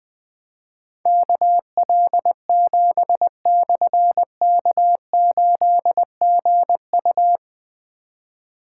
Morse code, heard as a keyed tone, about 20 wpm, 700 Hz. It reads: KL7/K8GU
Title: Difficult Call Signs 20wpm